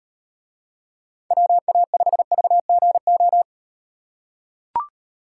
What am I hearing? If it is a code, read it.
Morse code, heard as a keyed tone, about 36 wpm, 700 Hz. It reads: WA5VGO